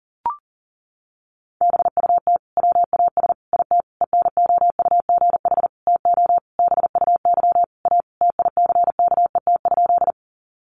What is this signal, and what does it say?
Morse code, heard as a keyed tone, about 40 wpm, 700 Hz. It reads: BUT WAS IT ENOUGH TO BUY A TICKET?
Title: Sentences from Top 1400 Words 40wpm